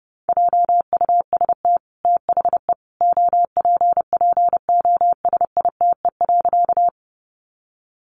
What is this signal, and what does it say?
Morse code, heard as a keyed tone, about 30 wpm, 700 Hz. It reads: JUST THE OPPOSITE.
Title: Sentences from Top 1300 Words 30wpm